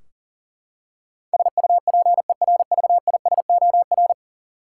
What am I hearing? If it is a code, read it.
Morse code, heard as a keyed tone, about 40 wpm, 700 Hz. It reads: SUPERVISOR
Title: Practice QSO - Exchange 3 - More Info - 40wpm